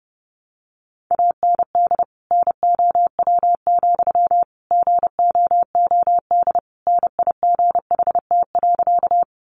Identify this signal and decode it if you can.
Morse code, heard as a keyed tone, about 30 wpm, 700 Hz. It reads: AND NOW, GOOD NIGHT.
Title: Sentences from Top 300 Words 30wpm